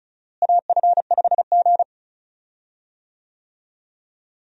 Morse code, heard as a keyed tone, about 34 wpm, 700 Hz. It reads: AF5G
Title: USA Call Signs - 4 Characters - Repeated 2x - 35wpm